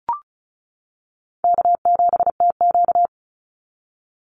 Morse code, heard as a keyed tone, about 34 wpm, 700 Hz. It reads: K7TQ
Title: NA SOTA Chaser Callsigns 35wpm